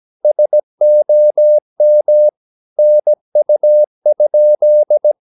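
Morse code, heard as a keyed tone, about 17 wpm, 600 Hz. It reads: SOM NU?
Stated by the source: Slow (QRS) Morse practice at 17 wpm, from CW med Gnister.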